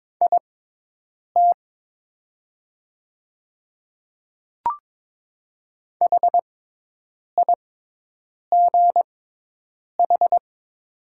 Morse code, heard as a keyed tone, about 22 wpm, spaced as for 8 wpm, 700 Hz. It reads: IT HIGH
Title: 6x Character Spacing - Top 500 Words - 22wpm